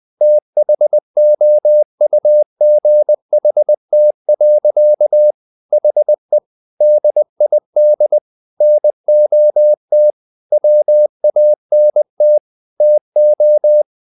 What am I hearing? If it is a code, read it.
Morse code, heard as a keyed tone, about 20 wpm, 600 Hz. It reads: THOUGHT. HE DID NOT WANT TO
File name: the old man and the sea part 2 20wpm